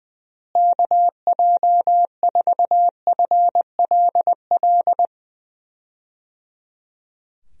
Morse code, heard as a keyed tone, about 20 wpm, 700 Hz. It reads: KJ4FLL
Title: USA Call Signs - 6 Characters - Repeated 3x - 20wpm